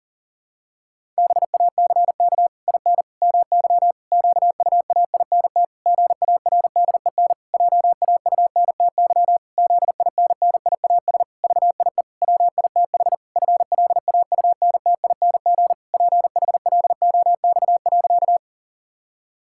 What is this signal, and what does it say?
Morse code, heard as a keyed tone, about 40 wpm, 700 Hz. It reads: BACK IN MY QUAINT GARDEN JAUNTY ZINNIAS VIE WITH FLAUNTING PHLOX.
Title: Pangram Challenge v2 - 40wpm